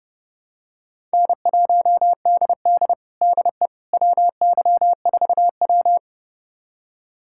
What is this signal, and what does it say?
Morse code, heard as a keyed tone, about 30 wpm, 700 Hz. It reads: N1DD DE WY4W
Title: USA Callsign DE Callsign - 4 Characters - 30wpm